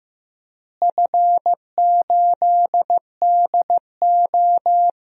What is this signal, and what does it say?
Morse code, heard as a keyed tone, about 15 wpm, 700 Hz. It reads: F8DO
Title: DXCC-100 Call Signs - 4 Characters - Repeated 3x - 15wpm